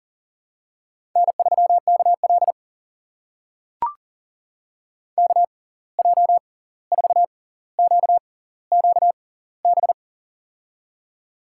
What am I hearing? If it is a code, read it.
Morse code, heard as a keyed tone, about 40 wpm, spaced as for 18 wpm, 700 Hz. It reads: N3KL KJ4QQB
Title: USA Call Signs - 6x Character Spacing - 40wpm